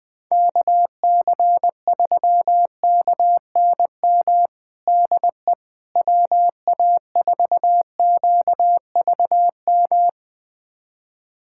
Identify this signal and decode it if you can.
Morse code, heard as a keyed tone, about 21 wpm, 700 Hz. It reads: KC3KNM DE WA4QVM
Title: USA Callsign DE Callsign - 6 Characters - 20wpm